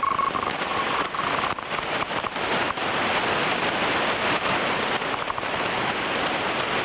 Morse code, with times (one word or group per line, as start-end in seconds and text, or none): none